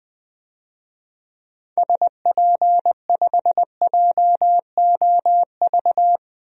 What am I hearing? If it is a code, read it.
Morse code, heard as a keyed tone, about 20 wpm, 700 Hz. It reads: SP5JOV
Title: DXCC-100 Call Signs - 6 Characters - 20wpm